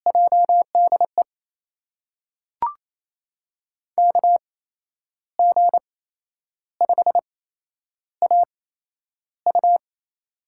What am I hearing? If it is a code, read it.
Morse code, heard as a keyed tone, about 28 wpm, spaced as for 10 wpm, 700 Hz. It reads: JDE KG5AU
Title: USA Call Signs - 5 Characters - 8x Character Spacing - 28wpm